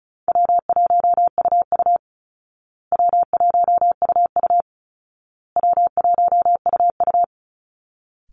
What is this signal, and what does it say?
Morse code, heard as a keyed tone, about 34 wpm, 700 Hz. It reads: W1UU W1UU W1UU
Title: Easy Call Signs Repeated 3x - 35wpm